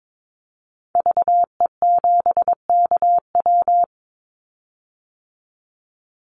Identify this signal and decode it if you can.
Morse code, heard as a keyed tone, about 22 wpm, 700 Hz. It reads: VE7KW
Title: Call Signs - 22wpm